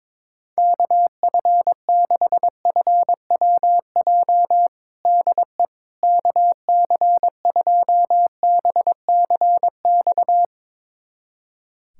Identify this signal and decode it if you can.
Morse code, heard as a keyed tone, about 21 wpm, 700 Hz. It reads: KF6FWJ DE KC2BCX
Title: USA Callsign DE Callsign - 6 Characters - 22wpm